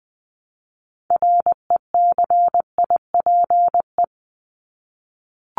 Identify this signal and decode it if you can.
Morse code, heard as a keyed tone, about 20 wpm, 700 Hz. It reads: RECIPE
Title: Top 1001-1250 Words 20wpm